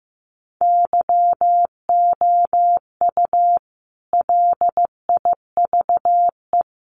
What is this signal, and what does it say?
Morse code, heard as a keyed tone, about 15 wpm, 700 Hz. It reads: YOU LIVE
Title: Sentences from Top 1000 Words 15wpm